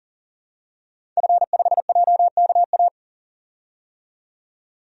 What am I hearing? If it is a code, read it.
Morse code, heard as a keyed tone, about 40 wpm, 700 Hz. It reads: F5JKA
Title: DXCC-100 Call Signs - 5 Characters - 40wpm